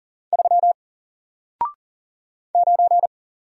Morse code, heard as a keyed tone, about 40 wpm, 700 Hz. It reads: 3 9
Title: Single Letter-Number Alt 40wpm